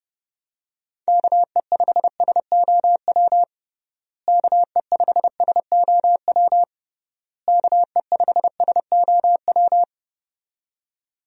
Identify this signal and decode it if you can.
Morse code, heard as a keyed tone, about 30 wpm, 700 Hz. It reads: KE5SOW KE5SOW KE5SOW
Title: USA Call Signs - 6 Characters - Repeated 3x - 30wpm